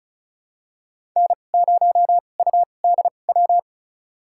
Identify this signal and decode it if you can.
Morse code, heard as a keyed tone, about 36 wpm, spaced as for 25 wpm, 700 Hz. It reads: N0UDW
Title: USA Call Signs - 5 Characters - 2x Character Spacing - 35wpm